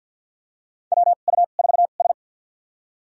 Morse code, heard as a keyed tone, about 53 wpm, spaced as for 37 wpm, 700 Hz. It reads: WU4S
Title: USA Call Signs - 2x Character Spacing - 50wpm